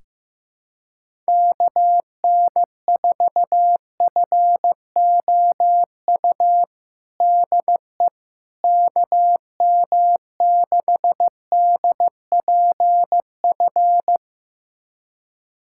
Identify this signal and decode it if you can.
Morse code, heard as a keyed tone, about 15 wpm, 700 Hz. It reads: KN4FOU DE KM6DPF
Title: USA Callsign DE Callsign - 6 Characters - 15wpm